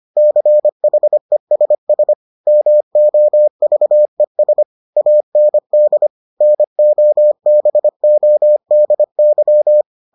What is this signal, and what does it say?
Morse code, heard as a keyed tone, about 25 wpm, 600 Hz. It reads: CHESS MOVES AND NOBODY